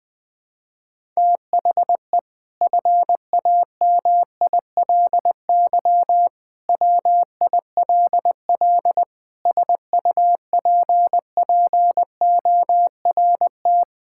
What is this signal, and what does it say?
Morse code, heard as a keyed tone, about 20 wpm, 700 Hz. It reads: THE FAMILY WILL SUPPORT
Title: Sentences from Top 1000 Words - Review Part 5 - 20wpm